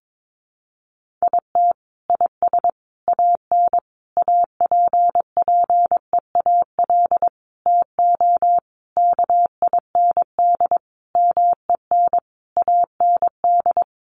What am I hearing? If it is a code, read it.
Morse code, heard as a keyed tone, about 22 wpm, 700 Hz. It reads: IT IS AN APPEAL TO KIND MEN AND
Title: Sentences from Top 1300 Words 22wpm